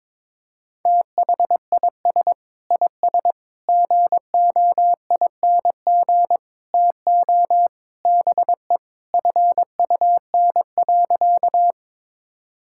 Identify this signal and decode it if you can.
Morse code, heard as a keyed tone, about 22 wpm, 700 Hz. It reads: THIS IS GOING TO BE FUN.
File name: practice / 22wpm / Sentences with 4-Word Phrases - 22wpm